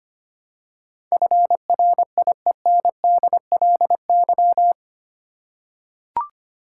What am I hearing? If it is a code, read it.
Morse code, heard as a keyed tone, about 25 wpm, 700 Hz. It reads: FRIENDLY
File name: Eight Letter Words - 25wpm